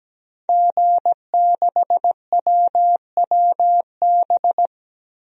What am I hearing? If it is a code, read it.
Morse code, heard as a keyed tone, about 17 wpm, 700 Hz. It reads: G6WWB
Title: DXCC-100 Call Signs - 5 Characters - Repeated 2x - 17wpm